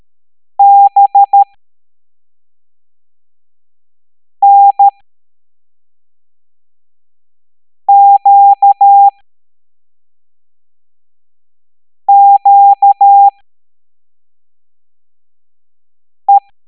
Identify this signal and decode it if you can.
Morse code, heard as a keyed tone, about 13 wpm, spaced as for 3 wpm, 800 Hz. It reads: BNQQE
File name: hamdata / L09-EX04